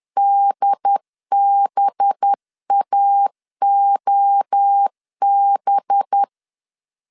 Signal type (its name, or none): Morse code